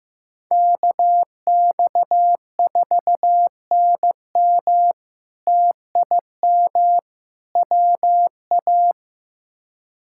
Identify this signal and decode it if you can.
Morse code, heard as a keyed tone, about 15 wpm, 700 Hz. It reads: KX4NM TIM WA
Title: K1USN SST QSO Practice - 15wpm